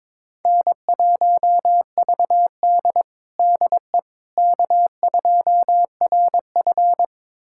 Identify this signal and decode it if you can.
Morse code, heard as a keyed tone, about 22 wpm, 700 Hz. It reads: N1VD DE K2RF
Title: USA Callsign DE Callsign - 4 Characters - 22wpm